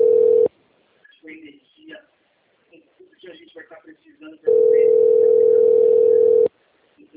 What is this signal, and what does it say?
Ringback tone (North America), 440 + 480 Hz, with speech.
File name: out- - - - - .44277